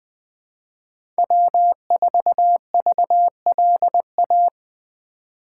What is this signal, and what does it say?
Morse code, heard as a keyed tone, about 20 wpm, 700 Hz. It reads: W4VLA
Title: USA Call Signs - 5 Characters - 8x Character Spacing - 20wpm